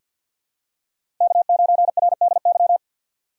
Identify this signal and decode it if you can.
Morse code, heard as a keyed tone, about 48 wpm, 700 Hz. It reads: K9RDY
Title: USA Call Signs - 2x Character Spacing - 50wpm